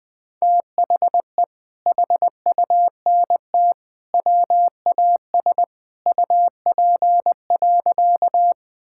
Morse code, heard as a keyed tone, about 19 wpm, 700 Hz. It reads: THE HUNT WAS UP.
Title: Sentences from Top 1600 Words 20wpm